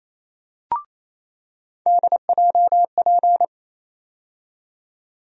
Morse code, heard as a keyed tone, about 28 wpm, 700 Hz. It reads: DJP
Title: USA Call Sign Postfixes - 3 Letters 28wpm